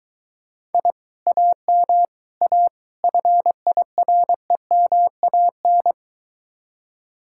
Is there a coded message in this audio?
Yes: Morse code from 0.7 to 5.9 s, heard as a keyed tone, about 23 wpm, 700 Hz. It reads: I AM A FIREMAN